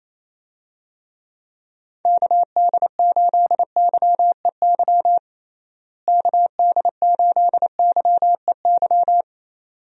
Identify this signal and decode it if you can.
Morse code, heard as a keyed tone, about 28 wpm, 700 Hz. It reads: KD8YEY KD8YEY
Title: USA Call Signs - 6 Characters - Repeated 2x - 28wpm